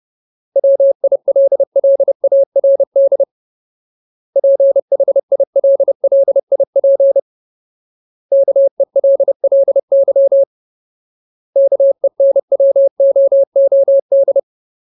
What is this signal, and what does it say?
Morse code, heard as a keyed tone, about 30 wpm, 550 Hz. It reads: WILLARD PHILLIP KELLY KENWOOD
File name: Ultimate Word List / 30WPM Ultimate